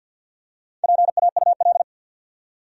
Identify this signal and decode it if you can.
Morse code, heard as a keyed tone, about 49 wpm, 700 Hz. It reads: PAUL
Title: USA Common Names - Max 4 Letters - 50wpm